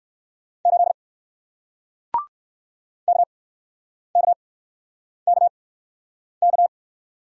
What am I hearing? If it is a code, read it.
Morse code, heard as a keyed tone, about 61 wpm, spaced as for 9 wpm, 700 Hz. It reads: 8 KKKK